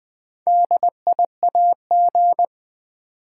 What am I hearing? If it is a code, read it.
Morse code, heard as a keyed tone, about 20 wpm, 690 Hz. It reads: DIAG
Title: Build A Word - 2 Letters At A Time - 8-Letter Words - 20wpm